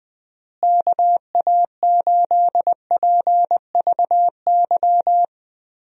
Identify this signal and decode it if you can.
Morse code, heard as a keyed tone, about 20 wpm, 700 Hz. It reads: KA8PVY
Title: USA Call Signs - 6 Characters - Repeated 3x - 20wpm